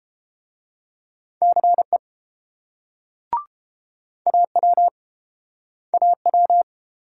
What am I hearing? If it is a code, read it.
Morse code, heard as a keyed tone, about 28 wpm, 700 Hz. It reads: CE AW AW